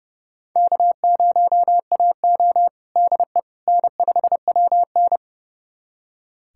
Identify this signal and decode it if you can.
Morse code, heard as a keyed tone, about 29 wpm, 700 Hz. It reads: K0AO DE N5WN
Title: USA Callsign DE Callsign - 4 Characters - 30wpm